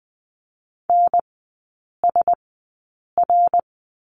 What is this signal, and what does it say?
Morse code, heard as a keyed tone, about 20 wpm, spaced as for 8 wpm, 700 Hz. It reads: NSR
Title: 053 Lesson 10e - Letter r - Three Characters 20wpm